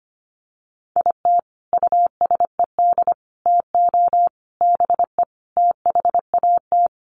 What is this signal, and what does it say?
Morse code, heard as a keyed tone, about 25 wpm, 700 Hz. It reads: IT USED TO BE THAT
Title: Sets of 5 Words Encore2 25wpm